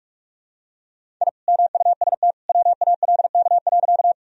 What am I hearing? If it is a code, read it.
Morse code, heard as a keyed tone, about 42 wpm, 700 Hz. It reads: I MUST WALK.
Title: Sentences from Top 1000 Words - Review Part 1 - 45wpm